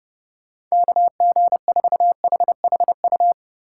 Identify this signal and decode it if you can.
Morse code, heard as a keyed tone, about 31 wpm, 700 Hz. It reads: KG4HHU